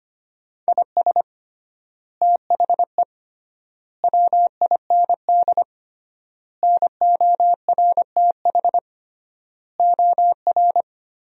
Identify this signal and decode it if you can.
Morse code, heard as a keyed tone, about 25 wpm, 700 Hz. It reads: IS THE WIND NORTH OR
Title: 3x Word-Spacing - Sentences from Top 400 Words - 25wpm